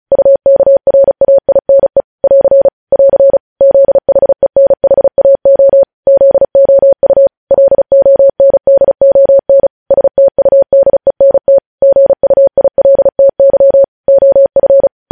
Morse code, heard as a keyed tone, about 36 wpm, 550 Hz. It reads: UKRAINE <AR> <AR> ZHENHAO ZOU LONDON STUDENT GUILTY OF
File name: podcast / News headlines-1x-35x35